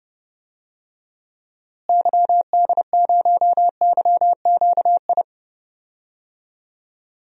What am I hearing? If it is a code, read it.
Morse code, heard as a keyed tone, about 31 wpm, 700 Hz. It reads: YD0YQI
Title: DXCC-100 Call Signs - 6 Characters - Repeated 2x - 30wpm